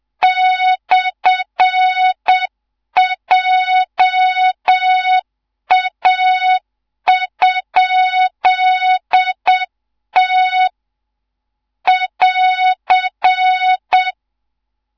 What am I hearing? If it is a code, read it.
Morse code, heard as a keyed tone, about 7 wpm, 3700 Hz, with noise. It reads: /JA?T <AR>
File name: Morze-HA2MN / hang / 14 280 HA2MN